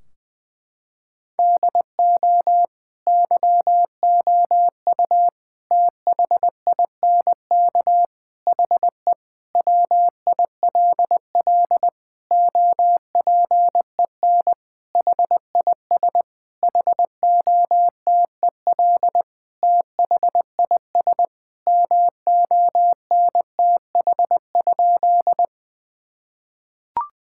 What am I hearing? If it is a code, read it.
Morse code, heard as a keyed tone, about 20 wpm, 700 Hz. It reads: DO YOU THINK HE WILL OPEN HIS HOTEL THIS MONTH?
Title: Sentences from Top 1600 Words 20wpm